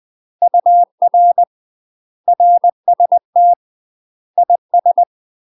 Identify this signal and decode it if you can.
Morse code, heard as a keyed tone, about 20 wpm, 700 Hz. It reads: UR RST IS
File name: UK MASTER QSOs TRAINING / 20WPM / UK MASTER QSOs 0060